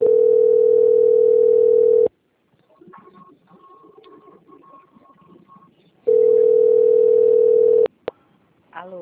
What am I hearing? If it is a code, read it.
Ringback tone (North America), 440 + 480 Hz.
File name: out- - - - - .18571